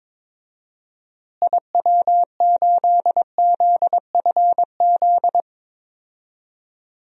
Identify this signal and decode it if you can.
Morse code, heard as a keyed tone, about 22 wpm, 700 Hz. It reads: IW8ZFZ
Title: DXCC-100 Call Signs - 6 Characters - Repeated 3x - 22wpm